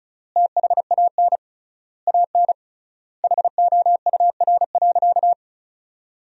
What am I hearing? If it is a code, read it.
Morse code, heard as a keyed tone, about 34 wpm, 700 Hz. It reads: THAN AN HOUR.
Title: 3x Word-Spacing - Sentences from Top 500 Words - 35wpm